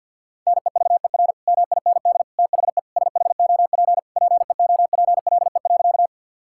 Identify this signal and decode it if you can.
Morse code, heard as a keyed tone, about 50 wpm, 700 Hz. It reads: NEVER MIND THE SHOP PEOPLE.
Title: Sentences from Top 1600 Words 50wpm